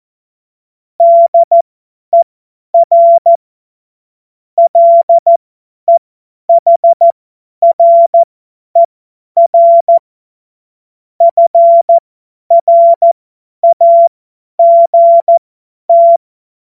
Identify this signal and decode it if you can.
Morse code, heard as a keyed tone, about 14 wpm, spaced as for 10 wpm, 680 Hz. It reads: DER LEHRER FRAGT
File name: Morsekurs / Witze / Witz 10. August 2021, 50 F70